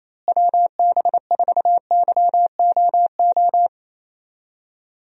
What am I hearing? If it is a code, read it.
Morse code, heard as a keyed tone, about 28 wpm, 700 Hz. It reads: WB4YOO
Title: USA Call Signs - 6 Characters - Repeated 3x - 28wpm